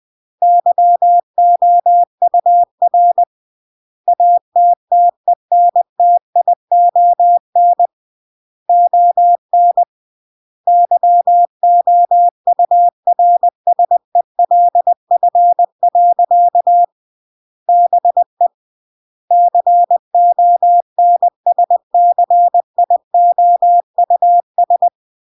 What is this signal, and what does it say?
Morse code, heard as a keyed tone, about 20 wpm, 700 Hz. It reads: YOUR ATTENTION ON YOURSELF. BE CONSCIOUS